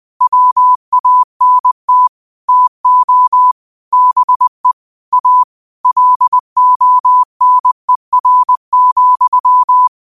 Morse code, heard as a keyed tone, about 20 wpm, 1000 Hz. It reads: WANT TO BE A LONER,